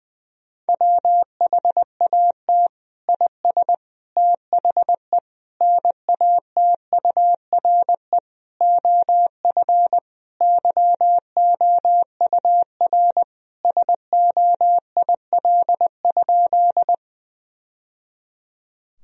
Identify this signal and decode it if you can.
Morse code, heard as a keyed tone, about 20 wpm, 700 Hz. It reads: WHAT IS THE NATURE OF YOUR SOIL?
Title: Sentences from Top 1000 Words - Review Part 4 - 20wpm